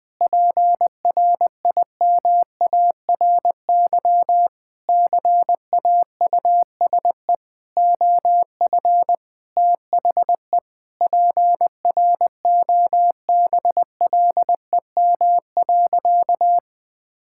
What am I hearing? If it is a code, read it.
Morse code, heard as a keyed tone, about 20 wpm, 700 Hz. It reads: PRIMARY CAUSE OF THE PROBLEM.
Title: Sentences from Top 1000 Words - Review Part 5 - 20wpm